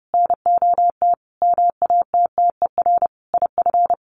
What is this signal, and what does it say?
Morse code, heard as a keyed tone, about 30 wpm, 700 Hz. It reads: NOT MATTER IF
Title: Sets of 5 Words Encore 30wpm